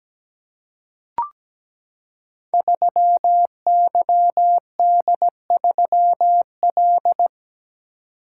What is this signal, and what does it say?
Morse code, heard as a keyed tone, about 17 wpm, 700 Hz. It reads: 3YD3L
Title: Code Groups - 5 Letter:Numbers - 17wpm